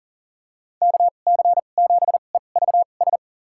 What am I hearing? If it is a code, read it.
Morse code, heard as a keyed tone, about 40 wpm, spaced as for 29 wpm, 700 Hz. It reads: KC7EVS
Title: USA Call Signs - 2x Character Spacing - 40wpm